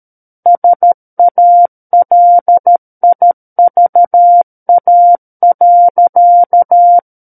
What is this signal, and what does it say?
Morse code, heard as a keyed tone, about 13 wpm, 700 Hz. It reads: SALIVA.
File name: Book One / 04-Chapter 4 - 13 WPM